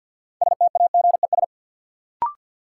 Morse code, heard as a keyed tone, about 50 wpm, 700 Hz. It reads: STAGES
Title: Sets of 5 Words Encore - New Word Drill - 50wpm